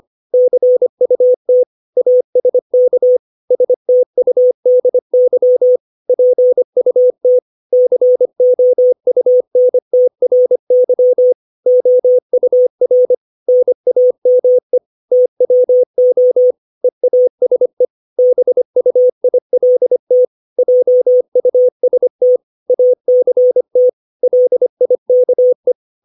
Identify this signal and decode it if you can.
Morse code, heard as a keyed tone, about 25 wpm, 500 Hz. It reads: CUT ASK STUDY PUT COUNTRY OUR NAME TWO EASE BUILT JUST ACT LIKE
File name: 632 words / 632 Words 25WPM